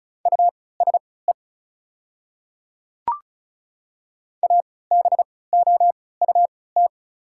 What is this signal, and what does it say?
Morse code, heard as a keyed tone, about 36 wpm, spaced as for 20 wpm, 690 Hz. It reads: USE ABOUT
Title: 3x Character Spacing - Top 100 Words - 35wpm